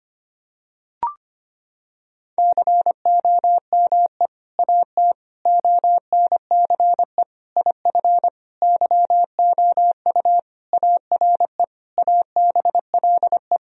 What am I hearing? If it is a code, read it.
Morse code, heard as a keyed tone, about 25 wpm, 700 Hz. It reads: COME AT ONCE IF YOU ARE ABLE
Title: Sentences from Top 1000 Words - Review Part 2 - 25wpm